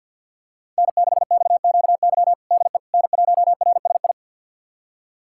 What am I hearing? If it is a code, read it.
Morse code, heard as a keyed tone, about 51 wpm, 700 Hz. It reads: N6XQY DE N1RSI
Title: USA Callsign DE Callsign - 5 Characters - 50wpm